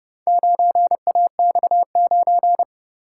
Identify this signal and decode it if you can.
Morse code, heard as a keyed tone, about 31 wpm, 700 Hz. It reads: 9AX9